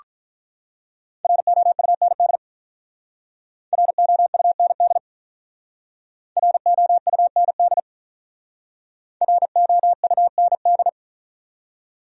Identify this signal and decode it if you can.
Morse code, heard as a keyed tone, about 47 wpm, 700 Hz. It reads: ROUND ROUND ROUND ROUND